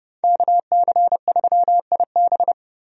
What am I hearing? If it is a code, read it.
Morse code, heard as a keyed tone, about 30 wpm, 700 Hz. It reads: KC3IB
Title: USA Call Signs - 5 Characters - Repeated 3x - 30wpm